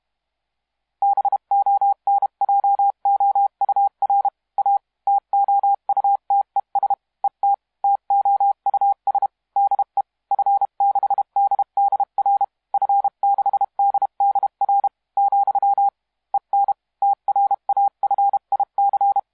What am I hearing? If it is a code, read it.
Morse code, heard as a keyed tone, about 31 wpm, 800 Hz. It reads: BONJOUR A TOUTES ET TOUS DE F6DDR F6DDR , EN TRAFIC
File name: musiqueradio / cw hsc demo f6ddr